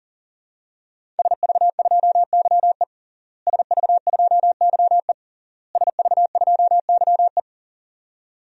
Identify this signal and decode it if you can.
Morse code, heard as a keyed tone, about 40 wpm, 700 Hz. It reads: SV2YE SV2YE SV2YE
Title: DXCC-100 Call Signs - 5 Characters - Repeated 3x - 40wpm